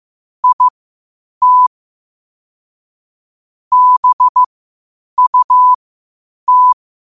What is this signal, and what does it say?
Morse code, heard as a keyed tone, about 15 wpm, spaced as for 8 wpm, 1000 Hz. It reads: IT BUT
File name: qotdfiles / 8wpm